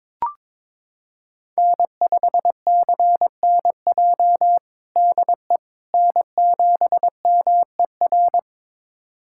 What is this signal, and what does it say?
Morse code, heard as a keyed tone, about 22 wpm, 700 Hz. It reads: N5CNJ DE N7MER
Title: USA Callsign DE Callsign - 5 Characters - 22wpm